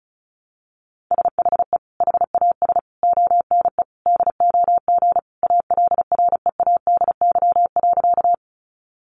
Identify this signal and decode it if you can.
Morse code, heard as a keyed tone, about 34 wpm, 700 Hz. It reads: SHE HAS ONE DOG ALREADY.
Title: Easy Phrases 35wpm